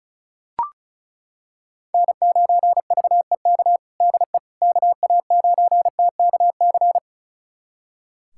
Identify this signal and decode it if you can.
Morse code, heard as a keyed tone, about 36 wpm, 700 Hz. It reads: N9VEK DE KA9TKC
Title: USA Callsign DE Callsign 35wpm